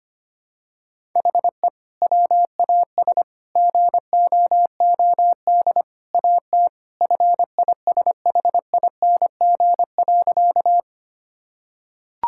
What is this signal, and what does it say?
Morse code, heard as a keyed tone, about 25 wpm, 700 Hz. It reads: HE WAS GOOD AT FISHING.
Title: Sentences from Top 1000 Words - Review Part 5 - 25wpm